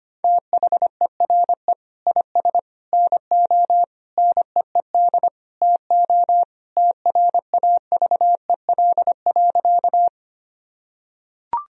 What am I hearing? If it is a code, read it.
Morse code, heard as a keyed tone, about 25 wpm, 700 Hz. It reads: THERE IS NO NEED TO TRAVEL.
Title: Sentences from Top 1000 Words - Review Part 2 - 25wpm